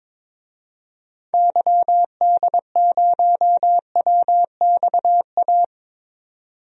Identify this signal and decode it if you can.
Morse code, heard as a keyed tone, about 22 wpm, 700 Hz. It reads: YD0WXA
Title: DXCC-100 Call Signs - 6 Characters - 22wpm